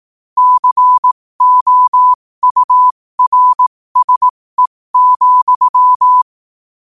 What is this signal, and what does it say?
Morse code, heard as a keyed tone, about 18 wpm, spaced as for 15 wpm, 1000 Hz. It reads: COURSE,
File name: qotdfiles / 15wpm